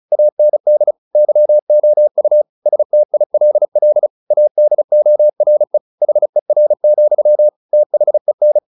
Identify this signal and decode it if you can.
Morse code, heard as a keyed tone, about 34 wpm, 600 Hz. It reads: AND YOU STILL ADORE HER, THEN